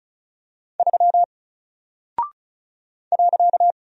Morse code, heard as a keyed tone, about 36 wpm, 700 Hz. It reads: 3 .
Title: Single Character (Letter:Number:Punctuation:Prosign) Alt 35wpm